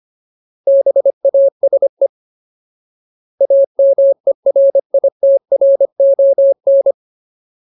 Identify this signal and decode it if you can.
Morse code, heard as a keyed tone, about 25 wpm, 550 Hz. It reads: BASE AMERITRON
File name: Ultimate Word List / 25WPM Ultimate